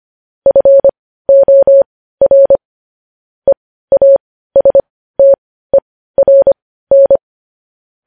Morse code, heard as a keyed tone, about 25 wpm, spaced as for 15 wpm, 550 Hz. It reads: FOR EASTERN
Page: news-headlines-in-morse-code-at-15-wpm